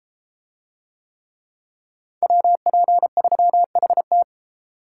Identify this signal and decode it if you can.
Morse code, heard as a keyed tone, about 31 wpm, 700 Hz. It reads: WP3HT